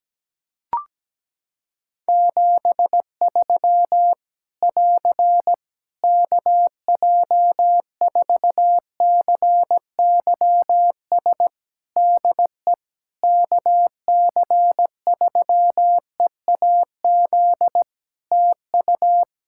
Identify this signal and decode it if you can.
Morse code, heard as a keyed tone, about 17 wpm, 700 Hz. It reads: 73 <AR> KJ4CYS DE KC3EAZ TU
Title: Practice QSO - Exchange 4 - Ending - 17wpm